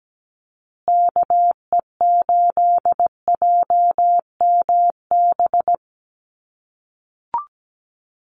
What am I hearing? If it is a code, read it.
Morse code, heard as a keyed tone, about 17 wpm, 700 Hz. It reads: KE8JMB
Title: USA Call Signs Repeated 3x 17wpm